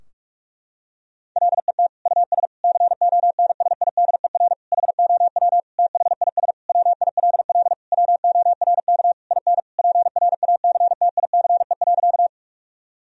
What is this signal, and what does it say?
Morse code, heard as a keyed tone, about 46 wpm, 700 Hz. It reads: LET US CONSIDER HOW THIS WILL WORK IN PRACTICE.
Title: Sentences from Top 1000 Words - Review Part 4 - 45wpm